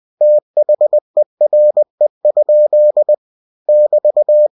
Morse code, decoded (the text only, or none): THERE? <BT>